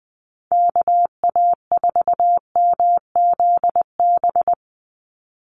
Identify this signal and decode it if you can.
Morse code, heard as a keyed tone, about 20 wpm, 700 Hz. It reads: KA4MZB
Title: USA Call Signs 20wpm